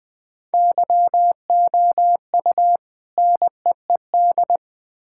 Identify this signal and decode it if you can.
Morse code, heard as a keyed tone, about 20 wpm, 700 Hz. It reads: YOU NEED
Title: Build A Phrase - 3-Word Phrases - 20wpm